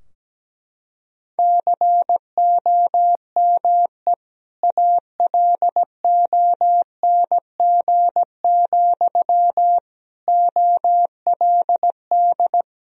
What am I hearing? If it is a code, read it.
Morse code, heard as a keyed tone, about 17 wpm, 700 Hz. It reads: COME ALONG, OLD
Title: Sentences from Top 1000 Words - Review Part 2 - 17wpm